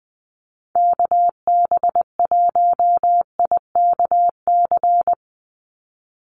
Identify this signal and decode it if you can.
Morse code, heard as a keyed tone, about 20 wpm, 700 Hz. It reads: KB1IKC
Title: USA Call Signs 20wpm